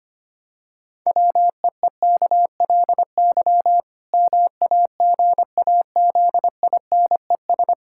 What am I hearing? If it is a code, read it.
Morse code, heard as a keyed tone, about 25 wpm, 700 Hz. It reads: WEEKLY MAGAZINES
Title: Pangram Challenge v2 - 25wpm